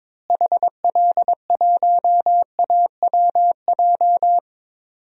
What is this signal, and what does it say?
Morse code, heard as a keyed tone, about 22 wpm, 700 Hz. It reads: HL1AWJ
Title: DXCC-100 Call Signs - 6 Characters - Repeated 2x - 22wpm